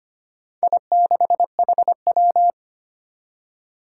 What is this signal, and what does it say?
Morse code, heard as a keyed tone, about 25 wpm, 700 Hz. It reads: I6HW
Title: Four Challenging Letter-Number Combinations 25wpm